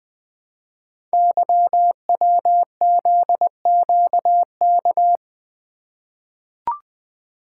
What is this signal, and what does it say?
Morse code, heard as a keyed tone, about 20 wpm, 700 Hz. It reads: YWZQK
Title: Code Groups - 5 Letters - 20wpm